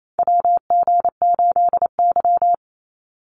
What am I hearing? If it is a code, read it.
Morse code, heard as a keyed tone, about 28 wpm, 700 Hz. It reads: WG8Y
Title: NA SOTA Chaser Callsigns 28wpm